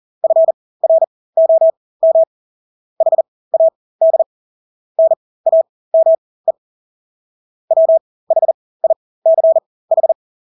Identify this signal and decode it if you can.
Morse code, heard as a keyed tone, about 40 wpm, spaced as for 20 wpm, 650 Hz. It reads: FROM HAD NAME WHICH
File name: Engelske ord 40-20 wpm